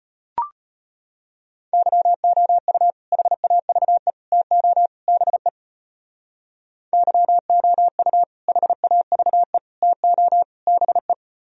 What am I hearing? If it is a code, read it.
Morse code, heard as a keyed tone, about 35 wpm, 700 Hz. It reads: YOU HAVE TO BE YOU HAVE TO BE